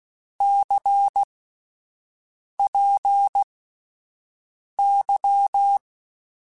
Morse code, heard as a keyed tone, about 16 wpm, spaced as for 5 wpm, 800 Hz. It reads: CPY
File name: (spaced) / QSOs Spaced-mp3s / WD0FBT 1 @10 sp